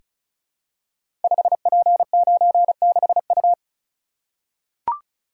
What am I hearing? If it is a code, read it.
Morse code, heard as a keyed tone, about 36 wpm, 700 Hz. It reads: 5P96U
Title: Code Groups - 5 Letter:Numbers - 35wpm